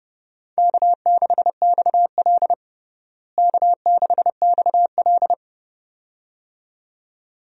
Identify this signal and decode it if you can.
Morse code, heard as a keyed tone, about 30 wpm, 700 Hz. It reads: K6XL K6XL
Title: USA Call Signs - 4 Characters - Repeated 2x - 30wpm